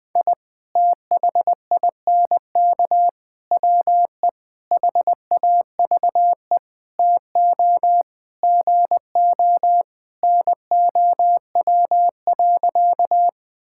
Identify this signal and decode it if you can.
Morse code, heard as a keyed tone, about 20 wpm, 700 Hz. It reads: I THINK WE HAVE TO GO NOW.
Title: Sentences with 5-Word Phrases - 20wpm